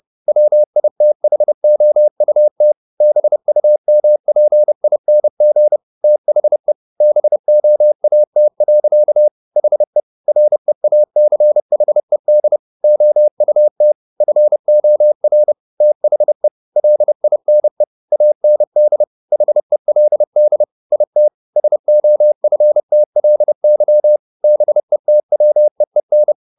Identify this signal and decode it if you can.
Morse code, heard as a keyed tone, about 30 wpm, 600 Hz. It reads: WITHOUT BUMPING THE BOAT. HE REACHED OUT FOR THE LINE AND HELD IT SOFTLY BETWEEN
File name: the old man and the sea / the old man and the sea 1 30wpm